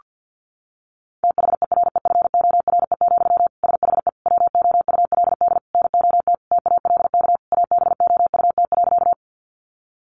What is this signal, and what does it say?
Morse code, heard as a keyed tone, about 50 wpm, 700 Hz. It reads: THEREFORE, SHE WOULD NOT TALK ABOUT.
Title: Sentences from Top 800 Words 50wpm